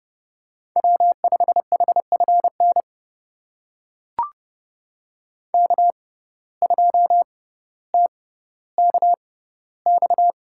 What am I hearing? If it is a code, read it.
Morse code, heard as a keyed tone, about 30 wpm, spaced as for 15 wpm, 700 Hz. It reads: W5HFN K2TKX
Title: USA Call Signs - 5 Characters - 6x Character Spacing - 30wpm